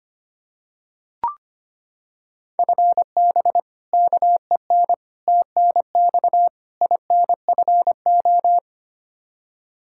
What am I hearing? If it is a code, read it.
Morse code, heard as a keyed tone, about 25 wpm, 700 Hz. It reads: FB KEN TNX INFO
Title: Practice QSO - Exchange 3 - More Info - 25wpm